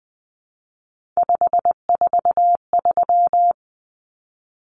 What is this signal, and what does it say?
Morse code, heard as a keyed tone, about 20 wpm, 700 Hz. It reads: 543
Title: Numbers - 3-Digit 20wpm